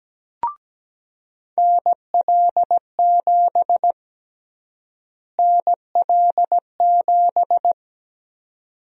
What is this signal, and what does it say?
Morse code, heard as a keyed tone, about 17 wpm, 700 Hz. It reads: NL7 NL7
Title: USA Call Sign Prefixes - Repeated 2x - 17wpm